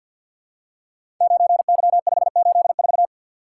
Morse code, heard as a keyed tone, about 52 wpm, 700 Hz. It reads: 9Y584
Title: Code Groups - 5 Letter:Numbers - 50wpm